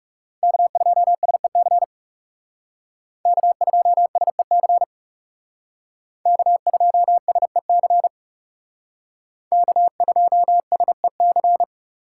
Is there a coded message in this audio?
Yes: Morse code from 0.4 to 11.6 s, heard as a keyed tone, about 38 wpm, 700 Hz. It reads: K2SEC K2SEC K2SEC K2SEC